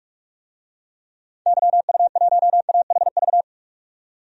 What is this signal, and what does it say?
Morse code, heard as a keyed tone, about 42 wpm, 700 Hz. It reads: YU1AHV
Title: DXCC-100 Call Signs - 6 Characters - 45wpm